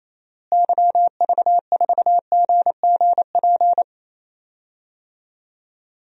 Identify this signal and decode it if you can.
Morse code, heard as a keyed tone, about 28 wpm, 700 Hz. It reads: YV4GGP
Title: DXCC-100 Call Signs - 6 Characters - Repeated 2x - 28wpm